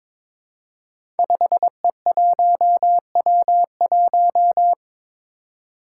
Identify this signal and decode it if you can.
Morse code, heard as a keyed tone, about 22 wpm, 700 Hz. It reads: 5E1W1
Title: Five Challenging Letter-Number Combinations 22wpm